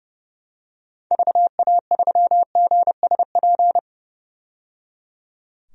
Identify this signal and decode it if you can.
Morse code, heard as a keyed tone, about 30 wpm, 700 Hz. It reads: VA3GSP
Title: DXCC-100 Call Signs - 6 Characters - 30wpm